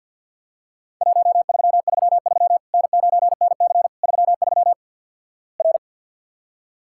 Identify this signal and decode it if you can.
Morse code, heard as a keyed tone, about 50 wpm, 700 Hz. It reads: 1333 N9NC 33 R
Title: Worked All Europe DX Contest - QTC Exchange - 50wpm